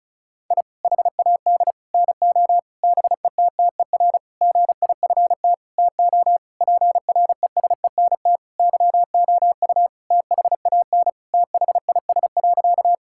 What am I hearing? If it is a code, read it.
Morse code, heard as a keyed tone, about 36 wpm, 700 Hz. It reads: I HAD NO BETTER GIFT TO PRESENT YOU THAN THIS.
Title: Sentences from Top 1000 Words - Review Part 6 - 35wpm